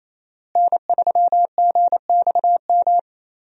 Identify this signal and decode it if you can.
Morse code, heard as a keyed tone, about 27 wpm, 700 Hz. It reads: N3GXM
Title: USA Call Signs - 5 Characters - Repeated 2x - 28wpm